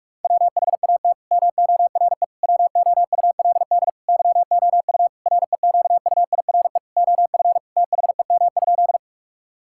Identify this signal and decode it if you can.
Morse code, heard as a keyed tone, about 42 wpm, 700 Hz. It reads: WHAT MORE WOULD YOU REQUIRE OF THEM?
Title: Sentences from Top 1000 Words - Review Part 6 - 45wpm